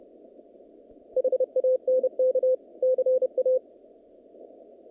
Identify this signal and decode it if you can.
Morse code, heard as a keyed tone, about 30 wpm, 510 Hz. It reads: HANK CA
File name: contests / naqp-cw-k5tr-op-k5pi- -2200